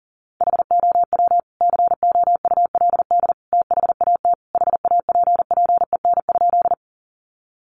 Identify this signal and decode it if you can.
Morse code, heard as a keyed tone, about 40 wpm, 700 Hz. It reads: HOW COULD THAT HAPPEN?
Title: Sentences from Top 400 Words 40wpm